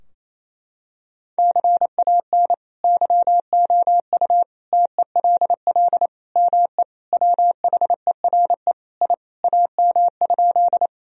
Morse code, heard as a keyed tone, about 28 wpm, 700 Hz. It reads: CAN YOU TELL ME WHERE I AM?
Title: Sentences from Top 1000 Words - Review Part 2 - 28wpm